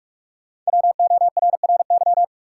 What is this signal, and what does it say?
Morse code, heard as a keyed tone, about 47 wpm, 700 Hz. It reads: WORRY
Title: Sentences from Top 1100 Words - New Word Drill - 45wpm